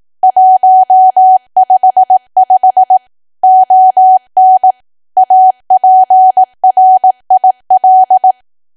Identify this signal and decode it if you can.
Morse code, heard as a keyed tone, about 18 wpm, 750 Hz. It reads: 155 ON APRIL
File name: cwa / W1AW-Level2 / 18WPM